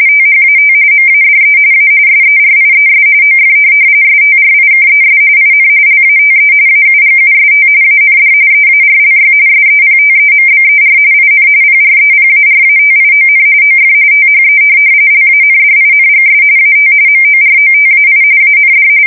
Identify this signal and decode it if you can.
RTTY, about 45.45 baud. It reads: THAT MADE A SERIES OF STRIKES IN LATE AUGUST AND EARLY SEPTEMBER IN PREPARATION OF THE LANDINGS AT MOROTAI AND